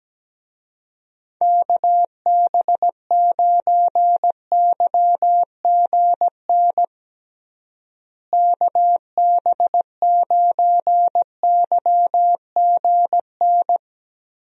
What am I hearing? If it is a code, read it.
Morse code, heard as a keyed tone, about 17 wpm, 700 Hz. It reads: KB9YGN KB9YGN
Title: USA Call Signs - 6 Characters - Repeated 3x - 17wpm